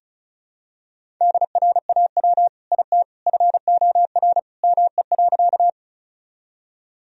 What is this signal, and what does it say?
Morse code, heard as a keyed tone, about 34 wpm, 700 Hz. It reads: DRAW IT FOR ME.
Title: Sentences from Top 1000 Words - Review Part 1 - 35wpm